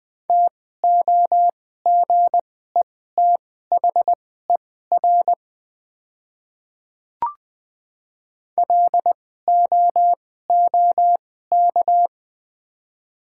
Morse code, heard as a keyed tone, about 20 wpm, spaced as for 15 wpm, 690 Hz. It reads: TOGETHER LOOK
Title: 2x Character Spacing - Top 500 Words - 20wpm